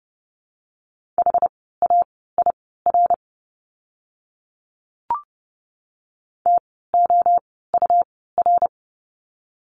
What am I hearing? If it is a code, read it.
Morse code, heard as a keyed tone, about 30 wpm, spaced as for 17 wpm, 700 Hz. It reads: HAIR TOUR
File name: Lesson 17f - Letter u - Words 30wpm